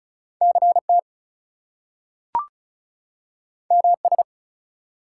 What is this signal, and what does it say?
Morse code, heard as a keyed tone, about 36 wpm, 700 Hz. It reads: CT MS
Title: US States and Canadian Provinces 35wpm